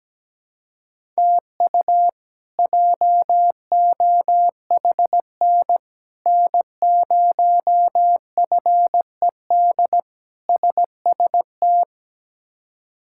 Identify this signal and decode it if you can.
Morse code, heard as a keyed tone, about 17 wpm, 700 Hz. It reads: TU JOHN N0FED SST
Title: K1USN SST QSO Practice - 17wpm